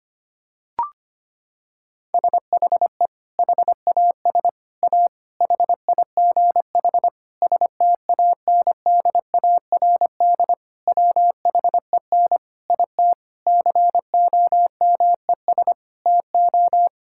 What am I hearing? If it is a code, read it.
Morse code, heard as a keyed tone, about 25 wpm, 700 Hz. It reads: SHE HAS A HIGH STANDARD WHEN IT COMES TO
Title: Sentences from Top 1000 Words - Review Part 6 - 25wpm